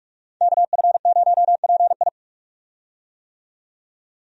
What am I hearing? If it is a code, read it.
Morse code, heard as a keyed tone, about 46 wpm, 700 Hz. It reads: KF0PI
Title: USA Call Signs - 5 Characters - 45wpm